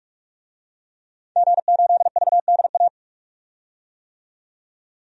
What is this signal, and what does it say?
Morse code, heard as a keyed tone, about 46 wpm, 700 Hz. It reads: G8VDA
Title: DXCC-100 Call Signs - 5 Characters - 45wpm